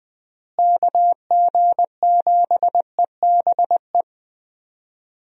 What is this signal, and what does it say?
Morse code, heard as a keyed tone, about 20 wpm, 700 Hz. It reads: KG7EBE
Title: USA Call Signs - 6 Characters - Repeated 3x - 20wpm